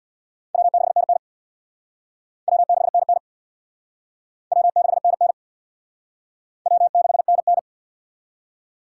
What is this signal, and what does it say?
Morse code, heard as a keyed tone, about 63 wpm, 700 Hz. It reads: W6NN W6NN W6NN W6NN